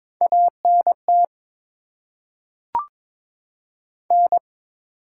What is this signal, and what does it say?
Morse code, heard as a keyed tone, about 22 wpm, 700 Hz. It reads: ANT N
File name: Top 20 QSO Elements - 8x Character Spacing - 22wpm